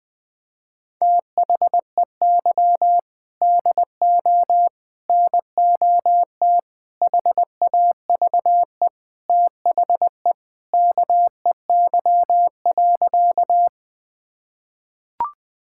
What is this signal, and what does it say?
Morse code, heard as a keyed tone, about 20 wpm, 700 Hz. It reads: THEY DO NOT HAVE THE KEY.
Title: Sentences with 5-Word Phrases - 20wpm